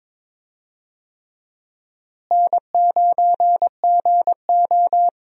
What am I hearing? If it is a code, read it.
Morse code, heard as a keyed tone, about 22 wpm, 700 Hz. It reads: N9GO
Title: USA Call Signs - 4 Characters - Repeated 3x - 22wpm